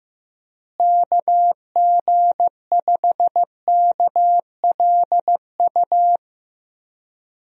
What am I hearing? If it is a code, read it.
Morse code, heard as a keyed tone, about 15 wpm, 700 Hz. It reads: KG5KLU
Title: USA Call Signs - 6 Characters - Repeated 2x - 15wpm